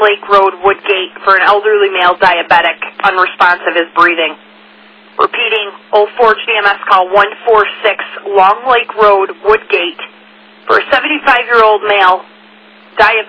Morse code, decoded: none

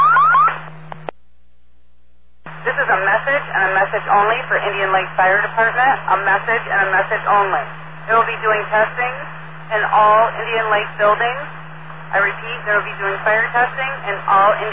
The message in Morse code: none

